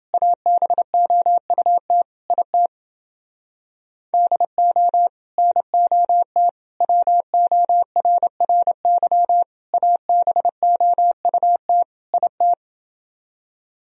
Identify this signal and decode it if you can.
Morse code, heard as a keyed tone, about 29 wpm, 700 Hz. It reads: ABOUT IT DO NOT WORRY ABOUT IT